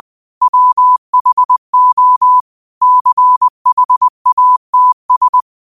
Morse code, heard as a keyed tone, about 20 wpm, 1000 Hz. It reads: WHO CHATS
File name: qotdfiles / 20wpm